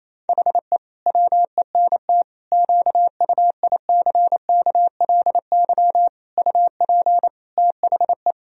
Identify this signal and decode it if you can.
Morse code, heard as a keyed tone, about 28 wpm, 700 Hz. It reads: HE WENT QUICKLY UP THE
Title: Sentences from Top 1000 Words - Review Part 2 - 28wpm